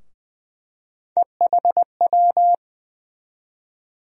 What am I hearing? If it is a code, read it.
Morse code, heard as a keyed tone, about 20 wpm, 700 Hz. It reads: EHW
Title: USA Call Sign Postfixes - 3 Letters 20wpm